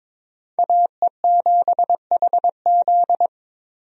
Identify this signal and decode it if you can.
Morse code, heard as a keyed tone, about 22 wpm, 700 Hz. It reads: AE7HZ
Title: USA Call Signs - 5 Characters - 22wpm